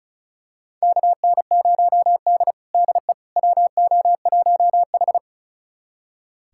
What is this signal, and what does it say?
Morse code, heard as a keyed tone, about 34 wpm, 700 Hz. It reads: KN0D DE WO1H
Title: USA Callsign DE Callsign - 4 Characters - 35wpm